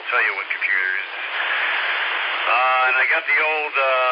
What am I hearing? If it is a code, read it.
Emergency Alert System (SAME) tones, heard as harsh buzzing bursts, with speech. It is heard from 0.6 to 3.4 s.